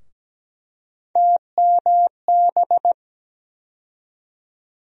Morse code, heard as a keyed tone, about 17 wpm, 700 Hz. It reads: TMB
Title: USA Call Sign Postfixes - 3 Letters 17wpm